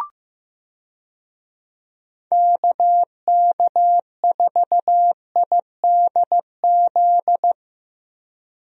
Morse code, heard as a keyed tone, about 15 wpm, 700 Hz. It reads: KK4IDZ
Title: USA Call Signs - 6 Characters - Repeated 2x - 15wpm